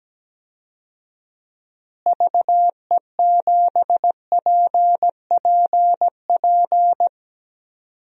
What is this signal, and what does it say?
Morse code, heard as a keyed tone, about 17 wpm, 700 Hz. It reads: VE7PPP
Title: DXCC-100 Call Signs - 6 Characters - Repeated 3x - 17wpm